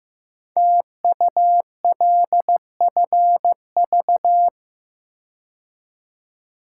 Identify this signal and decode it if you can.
Morse code, heard as a keyed tone, about 15 wpm, 700 Hz. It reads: TULFV
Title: Code Groups - 5 Letters - 15wpm